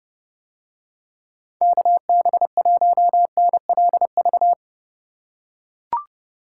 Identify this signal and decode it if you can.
Morse code, heard as a keyed tone, about 30 wpm, 700 Hz. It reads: KB1NLV